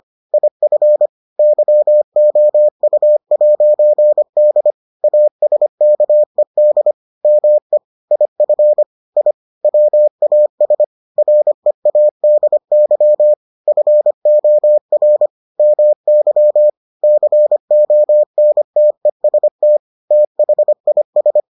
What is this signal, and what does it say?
Morse code, heard as a keyed tone, about 25 wpm, 600 Hz. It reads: IF YOU'D ASKED ME IF I WAS READY FOR MY CONTEST THIS